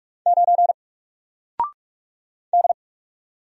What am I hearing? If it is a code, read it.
Morse code, heard as a keyed tone, about 44 wpm, 700 Hz. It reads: 9 D
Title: Single Letter-Number Alt 45wpm